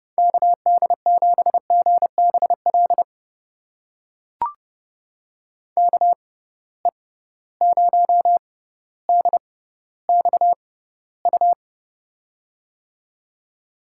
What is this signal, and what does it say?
Morse code, heard as a keyed tone, about 32 wpm, spaced as for 16 wpm, 700 Hz. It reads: KD7GBL KE0DXU
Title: USA Call Signs - 6 Characters - 6x Character Spacing - 30wpm